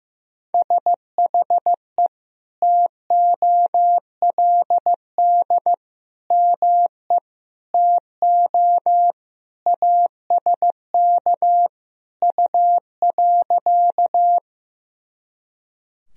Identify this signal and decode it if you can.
Morse code, heard as a keyed tone, about 16 wpm, 700 Hz. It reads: SHE TOLD ME TO ASK U.
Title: Sentences from Top 400 Words and Abbreviations 15wpm